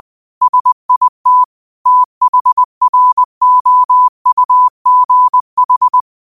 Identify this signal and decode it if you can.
Morse code, heard as a keyed tone, about 20 wpm, 1000 Hz. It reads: SIT THROUGH
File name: qotdfiles / 20wpm